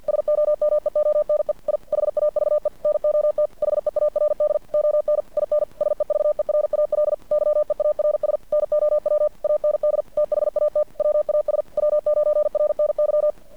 Morse code, heard as a keyed tone, about 50 wpm, 600 Hz. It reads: SOMEONE I HAVE NOT HEARD ON IN SEVERAL YEARS NOW AND THAT WAS W9RNY
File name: archived / CRSnet